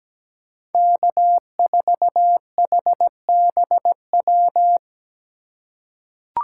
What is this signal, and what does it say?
Morse code, heard as a keyed tone, about 17 wpm, 700 Hz. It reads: K4HBW
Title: USA Call Signs - 5 Characters - 4x Character Spacing - 17wpm